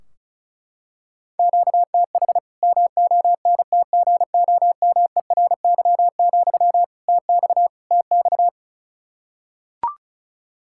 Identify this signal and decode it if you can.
Morse code, heard as a keyed tone, about 34 wpm, 700 Hz. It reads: QTH MONTGOMERY, TX TX
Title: Practice QSO - Exchange 1 - Beginning - 35wpm